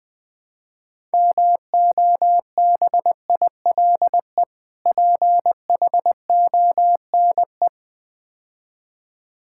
Morse code, heard as a keyed tone, about 20 wpm, 700 Hz. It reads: MOBILE PHONE
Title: Difficult Words - Speed-Racing with Hints - 20wpm